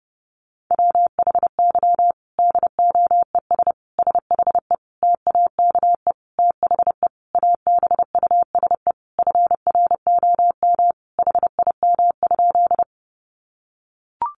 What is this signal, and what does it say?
Morse code, heard as a keyed tone, about 30 wpm, 700 Hz. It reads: WHY DOES SHE TAKE THE ABUSE FROM HIM?
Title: Sentences from Top 1300 Words 30wpm